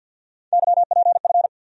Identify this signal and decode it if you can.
Morse code, heard as a keyed tone, about 53 wpm, 700 Hz. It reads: YPF
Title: USA Call Sign Postfixes - 3 Letters 50wpm